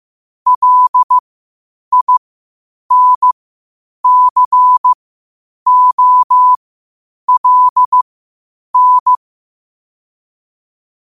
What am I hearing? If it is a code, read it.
Morse code, heard as a keyed tone, about 15 wpm, spaced as for 8 wpm, 1000 Hz. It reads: LINCOLN
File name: qotdfiles / 8wpm